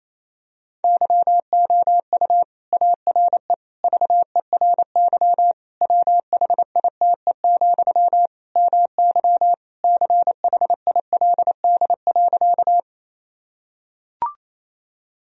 Morse code, heard as a keyed tone, about 28 wpm, 700 Hz. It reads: YOU ARE VERY WHITE, MY CHILD.
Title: Sentences from Top 1000 Words - Review Part 1 - 28wpm